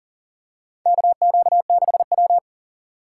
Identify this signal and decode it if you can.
Morse code, heard as a keyed tone, about 40 wpm, 700 Hz. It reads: KQ6W
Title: USA Call Signs - 3x Character Spacing - 40wpm